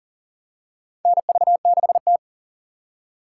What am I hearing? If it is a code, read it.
Morse code, heard as a keyed tone, about 40 wpm, 700 Hz. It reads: NV6T
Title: USA Call Signs - 4x Character Spacing - 40wpm